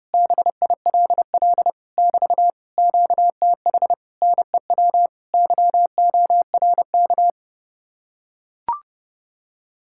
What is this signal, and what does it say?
Morse code, heard as a keyed tone, about 32 wpm, 700 Hz. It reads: BILL <BT> QTH NEW YORK